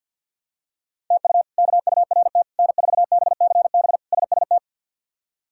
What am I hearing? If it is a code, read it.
Morse code, heard as a keyed tone, about 47 wpm, 700 Hz. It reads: TU KURT N4BCB SST
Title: K1USN SST QSO Practice - 50wpm